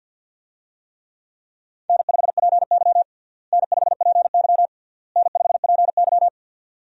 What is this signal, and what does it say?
Morse code, heard as a keyed tone, about 52 wpm, 700 Hz. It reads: N5PY N5PY N5PY
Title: USA Call Signs - 4 Characters - Repeated 3x - 50wpm